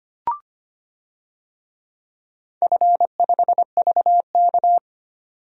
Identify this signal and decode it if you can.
Morse code, heard as a keyed tone, about 25 wpm, 700 Hz. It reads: F5VK
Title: DXCC-100 Call Signs - 4 Characters - 25wpm